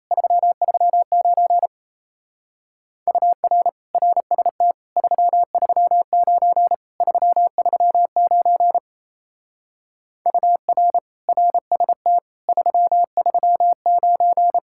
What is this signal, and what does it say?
Morse code, heard as a keyed tone, about 34 wpm, 700 Hz. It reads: 339 UR RST 339 339 UR RST 339
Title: Speed-Racing Common QSO Phrases - 28wpm